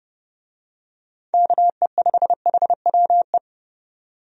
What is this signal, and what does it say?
Morse code, heard as a keyed tone, about 30 wpm, 700 Hz. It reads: KE5HWE
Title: USA Call Signs - 6 Characters - 30wpm